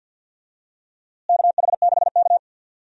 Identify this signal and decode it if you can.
Morse code, heard as a keyed tone, about 52 wpm, 700 Hz. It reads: KH6K
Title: DXCC-100 Call Signs - 4 Characters - Repeated 3x - 50wpm